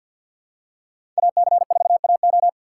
Morse code, heard as a keyed tone, about 50 wpm, 700 Hz. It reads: AC4AO
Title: USA Call Signs - 5 Characters - 2x Character Spacing - 50wpm